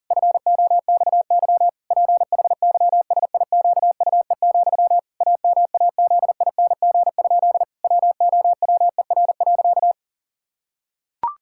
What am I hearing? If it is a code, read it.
Morse code, heard as a keyed tone, about 37 wpm, 700 Hz. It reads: FOXY PHYSIQUE, AMAZING? WOWER.
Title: Pangram Challenge v2 - 40wpm